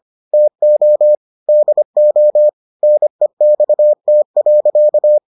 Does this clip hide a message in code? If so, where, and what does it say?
Yes: Morse code from 0.3 to 5.2 s, heard as a keyed tone, about 25 wpm, 600 Hz. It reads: TO DO NEXT.